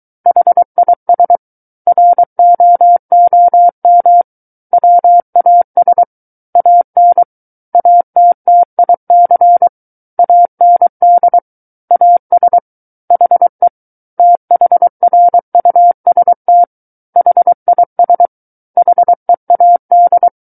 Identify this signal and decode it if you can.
Morse code, heard as a keyed tone, about 23 wpm, 700 Hz. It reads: HIS ROOM WAS AN ATTIC AND AS HE THRUST HIS HEAD
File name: Book One / 14-Chapter 14 - 23 WPM